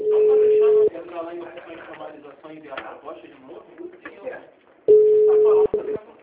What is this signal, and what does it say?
Ringback tone (Europe, 425 Hz), with speech.